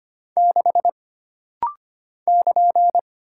Morse code, heard as a keyed tone, about 25 wpm, 700 Hz. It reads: 6 <KN>
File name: Single Character (Letter:Number:Punctuation:Prosign) - Rapid-Fire - 25wpm